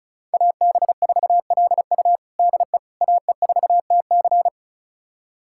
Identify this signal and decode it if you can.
Morse code, heard as a keyed tone, about 36 wpm, 700 Hz. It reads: AB4LU DE AE4TC
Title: USA Callsign DE Callsign - 5 Characters - 35wpm